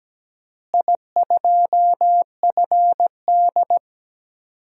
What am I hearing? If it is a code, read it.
Morse code, heard as a keyed tone, about 17 wpm, 700 Hz. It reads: I2FD
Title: DXCC-100 Call Signs - 4 Characters - Repeated 3x - 17wpm